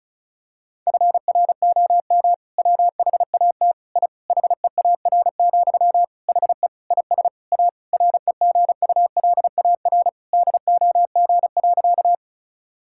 Morse code, heard as a keyed tone, about 36 wpm, 700 Hz. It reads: FROM WHAT I HEAR, HE IS A REGULAR DOG.
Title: Sentences from Top 1000 Words - Review Part 6 - 35wpm